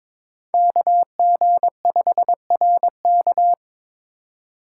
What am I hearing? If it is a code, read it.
Morse code, heard as a keyed tone, about 22 wpm, 700 Hz. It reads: KG5RK
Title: USA Call Signs - 5 Characters - 2x Character Spacing - 22wpm